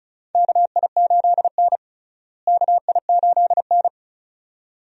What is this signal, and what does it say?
Morse code, heard as a keyed tone, about 36 wpm, 700 Hz. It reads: KI8N KI8N
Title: USA Call Signs - 4 Characters - Repeated 2x - 35wpm